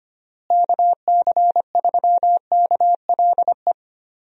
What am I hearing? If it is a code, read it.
Morse code, heard as a keyed tone, about 25 wpm, 700 Hz. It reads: KC3KLE
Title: USA Call Signs - 6 Characters - Repeated 3x - 25wpm